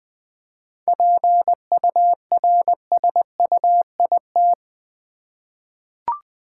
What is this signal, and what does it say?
Morse code, heard as a keyed tone, about 20 wpm, 700 Hz. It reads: PURSUIT
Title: 7 Letter Words - First 3 Letters Given - 20wpm